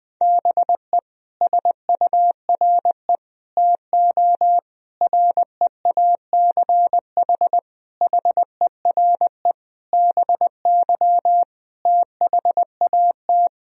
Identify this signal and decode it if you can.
Morse code, heard as a keyed tone, about 20 wpm, 700 Hz. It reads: BE SURE TO REACH HERE BY THAT
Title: Sentences from Top 1000 Words - Review Part 2 - 20wpm